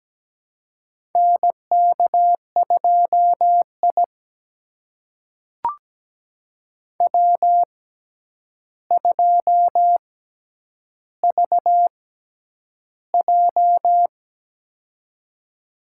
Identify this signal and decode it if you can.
Morse code, heard as a keyed tone, about 17 wpm, spaced as for 9 wpm, 700 Hz. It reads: NK2I W2VJ
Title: USA Call Signs - 4 Characters - 6x Character Spacing - 17wpm